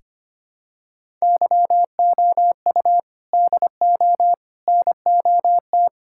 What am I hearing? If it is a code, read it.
Morse code, heard as a keyed tone, about 25 wpm, 700 Hz. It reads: YOU DO NOT
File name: practice / 25wpm / Sentences with 4-Word Phrases - 25wpm